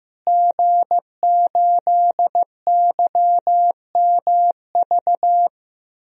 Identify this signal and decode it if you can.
Morse code, heard as a keyed tone, about 15 wpm, 700 Hz. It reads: G8YMV
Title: DXCC-100 Call Signs - 5 Characters - Repeated 2x - 15wpm